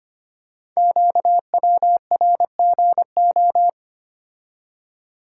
Morse code, heard as a keyed tone, about 25 wpm, 700 Hz. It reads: QWRGO
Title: Code Groups - 5 Letters - 25wpm